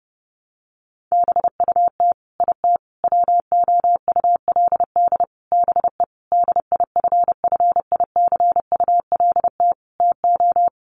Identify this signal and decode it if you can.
Morse code, heard as a keyed tone, about 30 wpm, 700 Hz. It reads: BUT IT WOULD BE DIFFICULT TO
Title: Sentences from Top 1400 Words 30wpm